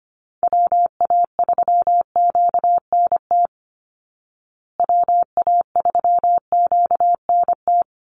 Morse code, heard as a keyed tone, about 25 wpm, 700 Hz. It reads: WA3QNT WA3QNT
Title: Call Signs Repeated 2x - 25wpm